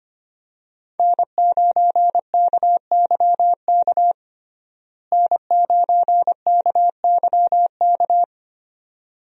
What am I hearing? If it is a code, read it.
Morse code, heard as a keyed tone, about 25 wpm, 700 Hz. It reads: N9KYK N9KYK
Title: USA Call Signs - 5 Characters - Repeated 2x - 25wpm